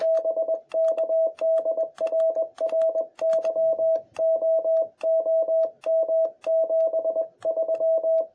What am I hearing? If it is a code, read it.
Morse code, heard as a keyed tone, about 21 wpm, 650 Hz. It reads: BK DR RYO OM 73
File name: jh8cbh / 2021cw